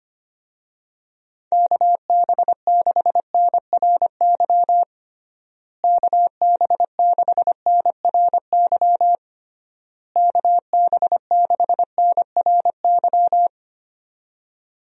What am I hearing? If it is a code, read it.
Morse code, heard as a keyed tone, about 25 wpm, 700 Hz. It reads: KB6NRY KB6NRY KB6NRY
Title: USA Call Signs - 6 Characters - Repeated 3x - 25wpm